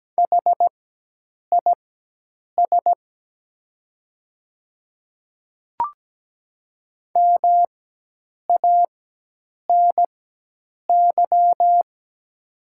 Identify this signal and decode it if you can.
Morse code, heard as a keyed tone, about 17 wpm, spaced as for 8 wpm, 700 Hz. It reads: HIS MANY
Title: 4x Character Spacing - Top 100 Words - 17wpm